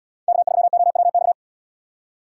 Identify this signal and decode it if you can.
Morse code, heard as a keyed tone, about 74 wpm, 700 Hz. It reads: K3OWX